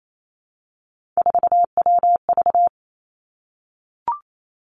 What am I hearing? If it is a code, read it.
Morse code, heard as a keyed tone, about 28 wpm, 700 Hz. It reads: 4WV
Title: Three Challenging Letter-Number Combinations 28wpm